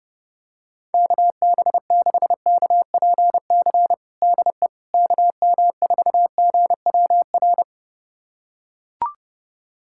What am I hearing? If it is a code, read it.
Morse code, heard as a keyed tone, about 29 wpm, 700 Hz. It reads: KB6KPC DE KM4GWR
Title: USA Callsign DE Callsign - 6 Characters - 30wpm